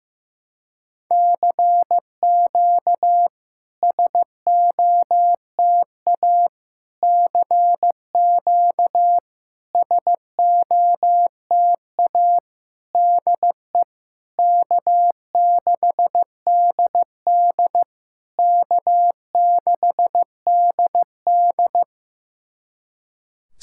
Morse code, heard as a keyed tone, about 16 wpm, 700 Hz. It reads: CQ SOTA CQ SOTA DE K6DD K6DD
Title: SOTA QSO Practice 15wpm